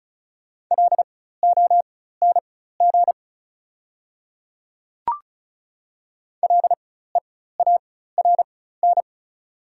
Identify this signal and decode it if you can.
Morse code, heard as a keyed tone, about 36 wpm, spaced as for 16 wpm, 700 Hz. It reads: LONG LEARN
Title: 4x Character Spacing - Top 300 Words - 35wpm